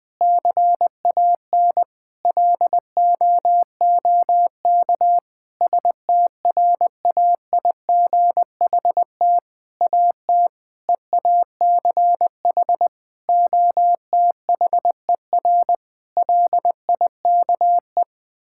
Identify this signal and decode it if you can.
Morse code, heard as a keyed tone, about 20 wpm, 700 Hz. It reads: CAN LOOK STRAIGHT AT EACH OTHER LIKE
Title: Sentences from Top 1000 Words - Review Part 6 - 20wpm